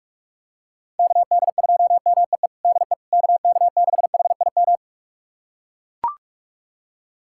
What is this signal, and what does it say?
Morse code, heard as a keyed tone, about 42 wpm, 700 Hz. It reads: KD2MEE DE KK6HIM
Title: USA Callsign DE Callsign - 6 Characters - 45wpm